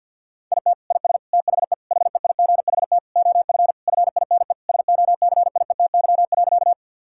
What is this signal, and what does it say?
Morse code, heard as a keyed tone, about 50 wpm, 700 Hz. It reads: IT IS THE HEIGHT OF FINE SOCIETY.
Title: Sentences from Top 1600 Words 50wpm